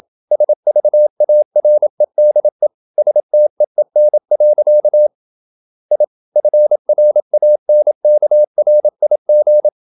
Morse code, heard as a keyed tone, about 27 wpm, 600 Hz. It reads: SVAREDE STEEN. I FRANKRIG
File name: Ferie QRQ 27wpm